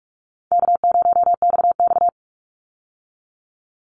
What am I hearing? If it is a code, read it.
Morse code, heard as a keyed tone, about 44 wpm, 700 Hz. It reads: K0XX
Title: Easy Call Signs Repeated 2x - 45wpm